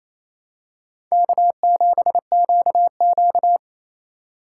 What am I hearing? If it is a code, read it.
Morse code, heard as a keyed tone, about 28 wpm, 700 Hz. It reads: K7QQ
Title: USA Call Signs - 3x Character Spacing - 28wpm